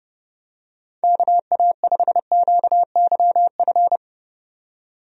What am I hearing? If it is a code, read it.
Morse code, heard as a keyed tone, about 30 wpm, 700 Hz. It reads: KA5QYF
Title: USA Call Signs - 6 Characters - 30wpm